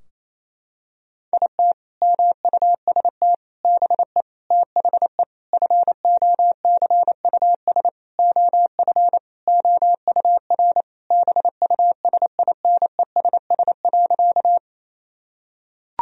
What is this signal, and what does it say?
Morse code, heard as a keyed tone, about 28 wpm, 700 Hz. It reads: IT MUST BE THE FOCUS OF OUR BUSINESS.
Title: Sentences from Top 1000 Words - Review Part 4 - 28wpm